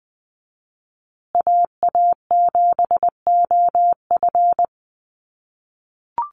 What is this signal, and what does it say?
Morse code, heard as a keyed tone, about 20 wpm, 700 Hz. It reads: AA7OF
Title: USA Call Signs Repeated 3x 20wpm